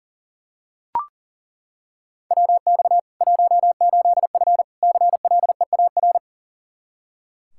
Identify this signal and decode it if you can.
Morse code, heard as a keyed tone, about 40 wpm, 700 Hz. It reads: WX 18F CLEAR
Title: Practice QSO - Exchange 2 - Info - 40wpm